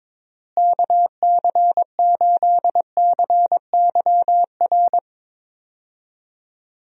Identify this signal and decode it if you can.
Morse code, heard as a keyed tone, about 22 wpm, 700 Hz. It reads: KC8CYR
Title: USA Call Signs - 6 Characters - Repeated 3x - 22wpm